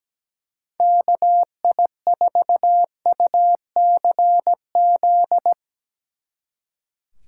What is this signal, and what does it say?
Morse code, heard as a keyed tone, about 17 wpm, 700 Hz. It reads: KI4UCZ
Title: Speed Racing - USA 2x3 Call Signs 17wpm